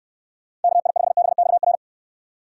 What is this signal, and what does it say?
Morse code, heard as a keyed tone, about 69 wpm, 700 Hz. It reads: KI5BXR